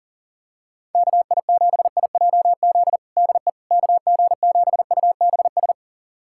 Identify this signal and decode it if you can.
Morse code, heard as a keyed tone, about 37 wpm, 700 Hz. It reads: KI7IJZ DE KG7UBS
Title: USA Callsign DE Callsign 40wpm